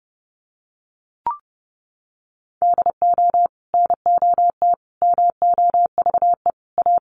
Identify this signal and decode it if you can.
Morse code, heard as a keyed tone, about 30 wpm, 700 Hz. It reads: DO NOT MOVE A
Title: Sentences from Top 1300 Words 30wpm